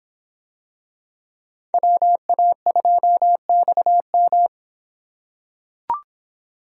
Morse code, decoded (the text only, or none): WA2XM